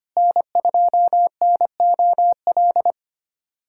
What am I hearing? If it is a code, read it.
Morse code, heard as a keyed tone, about 25 wpm, 700 Hz. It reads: N2NOL
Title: USA Callsign DE Callsign 25wpm